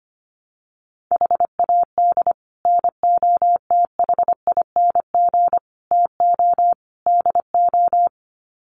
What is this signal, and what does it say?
Morse code, heard as a keyed tone, about 25 wpm, 700 Hz. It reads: HAD NOTHING TO DO
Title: Sets of 4 Words 25wpm